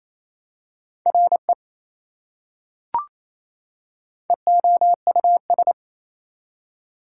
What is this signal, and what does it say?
Morse code, heard as a keyed tone, about 27 wpm, 700 Hz. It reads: RE EOUS
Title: Prefixes and Suffixes 28wpm